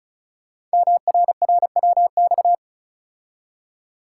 Morse code, heard as a keyed tone, about 36 wpm, 700 Hz. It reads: MRRWX
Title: Code Groups - 5 Letters - 35wpm